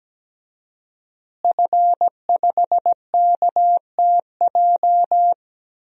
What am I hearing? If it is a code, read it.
Morse code, heard as a keyed tone, about 17 wpm, 700 Hz. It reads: F5KTJ
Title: DXCC-100 Call Signs - 5 Characters - Repeated 3x - 17wpm